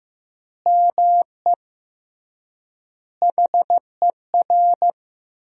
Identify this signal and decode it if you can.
Morse code, heard as a keyed tone, about 15 wpm, 700 Hz. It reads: ME HER
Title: 3x Word-Spacing - Sentences from Top 500 Words - 15wpm